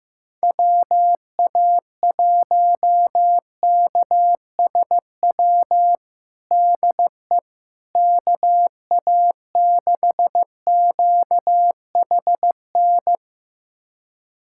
Morse code, heard as a keyed tone, about 15 wpm, 700 Hz. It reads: WA1KSW DE KA6QHN
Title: USA Callsign DE Callsign - 6 Characters - 15wpm